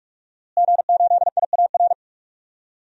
Morse code, heard as a keyed tone, about 43 wpm, 700 Hz. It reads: G8IAR
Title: DXCC-100 Call Signs - 5 Characters - Repeated 2x - 45wpm